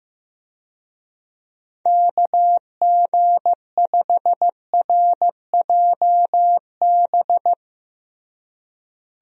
Morse code, heard as a keyed tone, about 15 wpm, 700 Hz. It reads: KG5RJB
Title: USA Call Signs - 6 Characters - Repeated 3x - 15wpm